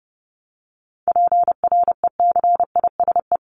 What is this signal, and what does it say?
Morse code, heard as a keyed tone, about 30 wpm, 700 Hz. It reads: PRECISE
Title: LY Suffix - Top 100 Words with Base Words 30wpm